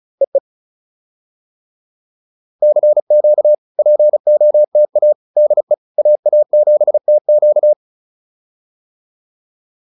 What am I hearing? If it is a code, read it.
Morse code, heard as a keyed tone, about 34 wpm, 590 Hz. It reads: EE CQ POTA DE AA7TQ